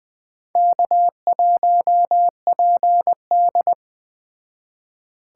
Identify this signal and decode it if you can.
Morse code, heard as a keyed tone, about 21 wpm, 700 Hz. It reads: K1PD
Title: Speed Racing - USA Call Signs 17wpm